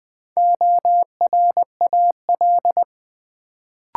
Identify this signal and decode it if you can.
Morse code, heard as a keyed tone, about 20 wpm, 700 Hz. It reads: ORAL
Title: Build A Word - 1 Letter At A Time - 3 to 5-Letter Words - 20wpm